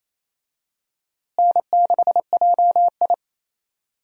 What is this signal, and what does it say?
Morse code, heard as a keyed tone, about 28 wpm, 700 Hz. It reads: N6JI
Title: USA Call Signs - 4 Characters - Repeated 3x - 28wpm